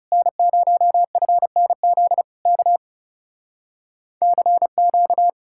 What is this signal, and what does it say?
Morse code, heard as a keyed tone, about 39 wpm, 700 Hz. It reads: N0FNZ K CQ